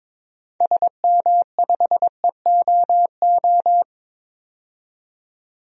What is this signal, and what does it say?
Morse code, heard as a keyed tone, about 22 wpm, 700 Hz. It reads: SM5EOO
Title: DXCC-100 Call Signs - 6 Characters - 22wpm